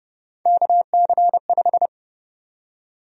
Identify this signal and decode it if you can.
Morse code, heard as a keyed tone, about 30 wpm, 700 Hz. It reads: KC5
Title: USA Call Sign Prefixes - 30wpm